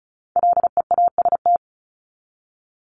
Morse code, heard as a keyed tone, about 34 wpm, 700 Hz. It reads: LEAST
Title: Top 501-750 Words 35wpm